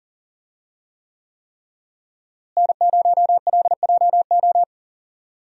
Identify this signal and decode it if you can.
Morse code, heard as a keyed tone, about 40 wpm, 700 Hz. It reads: N0LJO
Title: USA Call Signs - 5 Characters - Repeated 3x - 40wpm